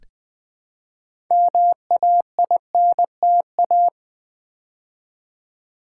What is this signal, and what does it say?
Morse code, heard as a keyed tone, about 20 wpm, 700 Hz. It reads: MAINTA
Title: Build A Word - 2 Letters At A Time - 8-Letter Words - 20wpm